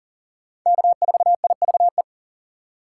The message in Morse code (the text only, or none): K4IVE